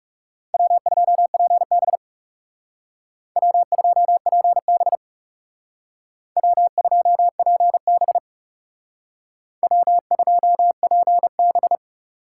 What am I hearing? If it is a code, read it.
Morse code, heard as a keyed tone, about 38 wpm, 700 Hz. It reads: W2PB W2PB W2PB W2PB